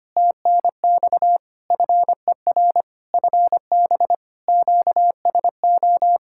Morse code, heard as a keyed tone, about 25 wpm, 700 Hz. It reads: TNX FER FB QSO
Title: Practice QSO - Exchange 4 - Ending - 25wpm